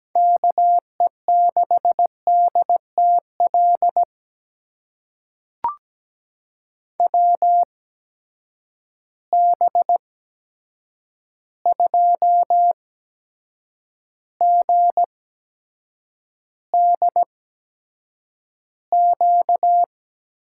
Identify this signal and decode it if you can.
Morse code, heard as a keyed tone, about 17 wpm, spaced as for 7 wpm, 700 Hz. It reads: KE6DTL WB2GDQ
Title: USA Call Signs - 6 Characters - 8x Character Spacing - 17wpm